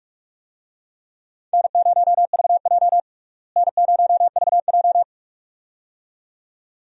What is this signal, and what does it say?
Morse code, heard as a keyed tone, about 43 wpm, 700 Hz. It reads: N0VJ N0VJ
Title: USA Call Signs - 4 Characters - Repeated 2x - 45wpm